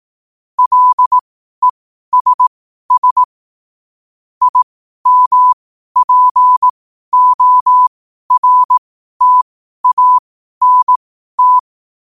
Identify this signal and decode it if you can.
Morse code, heard as a keyed tone, about 18 wpm, spaced as for 12 wpm, 1000 Hz. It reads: LESS IMPORTANT
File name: qotdfiles / 12wpm